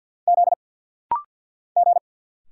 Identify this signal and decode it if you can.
Morse code, heard as a keyed tone, about 50 wpm, 700 Hz. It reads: Z G
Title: Single Letters - Rapid-Fire 50wpm